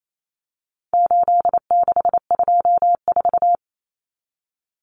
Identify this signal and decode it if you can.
Morse code, heard as a keyed tone, about 28 wpm, 700 Hz. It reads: 8624
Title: Common Cut Numbers - 4-Digit 28wpm